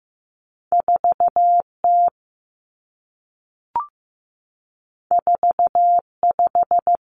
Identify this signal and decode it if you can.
Morse code, heard as a keyed tone, about 15 wpm, 700 Hz. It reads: 4 T 4 5
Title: Two Challenging Letter-Number Combinations 15wpm